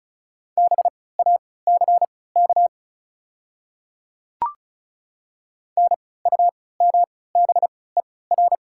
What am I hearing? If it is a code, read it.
Morse code, heard as a keyed tone, about 36 wpm, spaced as for 20 wpm, 690 Hz. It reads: BACK NUMBER
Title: 3x Character Spacing - Top 200 Words - 35wpm